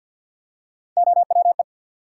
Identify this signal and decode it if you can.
Morse code, heard as a keyed tone, about 51 wpm, 700 Hz. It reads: OWE
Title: Three Letter Words - 50wpm